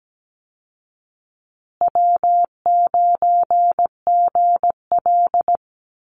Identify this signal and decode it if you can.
Morse code, heard as a keyed tone, about 17 wpm, 700 Hz. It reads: W9GL
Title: Call Signs Repeated 2x - 17wpm